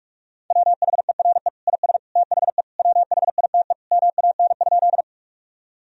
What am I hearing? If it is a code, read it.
Morse code, heard as a keyed tone, about 45 wpm, 700 Hz. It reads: WHERE IS THE WHITE MAN?
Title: Sentences from Top 1000 Words - Review Part 1 - 45wpm